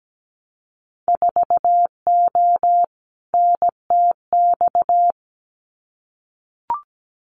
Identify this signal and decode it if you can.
Morse code, heard as a keyed tone, about 17 wpm, 700 Hz. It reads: 4O NTX
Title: Winter Field Day Exchanges 17wpm